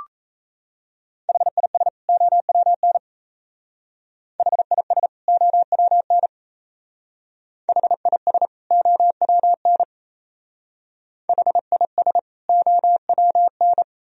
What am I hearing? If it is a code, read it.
Morse code, heard as a keyed tone, about 36 wpm, 700 Hz. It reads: HIS OWN HIS OWN HIS OWN HIS OWN